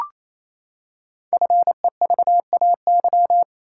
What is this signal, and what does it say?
Morse code, heard as a keyed tone, about 27 wpm, 700 Hz. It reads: FEVAY
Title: Code Groups - 5 Letter:Numbers - 28wpm